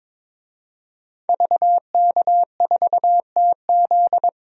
Morse code, heard as a keyed tone, about 22 wpm, 700 Hz. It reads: VK4TZ
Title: DXCC-100 Call Signs - 5 Characters - 22wpm